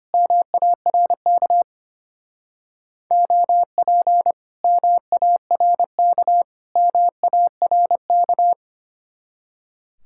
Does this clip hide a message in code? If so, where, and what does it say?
Yes: Morse code from 0.1 to 8.5 s, heard as a keyed tone, about 32 wpm, 700 Hz. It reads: MARK OP MARK MARK